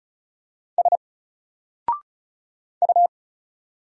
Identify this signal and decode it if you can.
Morse code, heard as a keyed tone, about 34 wpm, 700 Hz. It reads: S U
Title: Single Letter-Number Alt 35wpm